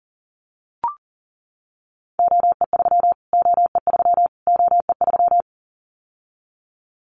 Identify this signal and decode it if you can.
Morse code, heard as a keyed tone, about 40 wpm, 700 Hz. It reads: OE3 OE3 OE3
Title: Callsign Prefixes for Top 25 DXCC - Repeated 3x - 40wpm